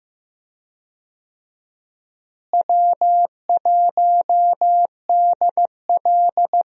Morse code, heard as a keyed tone, about 15 wpm, 700 Hz. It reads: W1DL
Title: USA Call Signs - 4 Characters - Repeated 2x - 15wpm